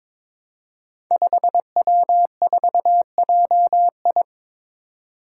Morse code, heard as a keyed tone, about 23 wpm, 700 Hz. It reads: 5W4JI
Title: Five Challenging Letter-Number Combinations 22wpm